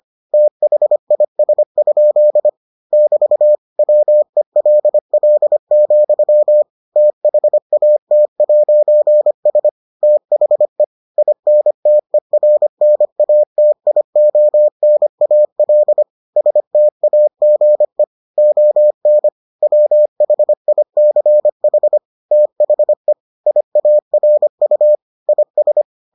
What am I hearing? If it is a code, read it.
Morse code, heard as a keyed tone, about 25 wpm, 600 Hz. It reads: THIS? <BT> WELL, THAT'S THE INTERNATIONAL STAGE ON WHICH THE IARU IS